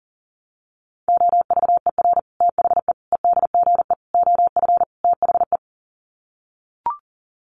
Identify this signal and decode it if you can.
Morse code, heard as a keyed tone, about 37 wpm, 700 Hz. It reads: OVER THE EDGE OF THE
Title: Sets of 5 Words Encore2 40wpm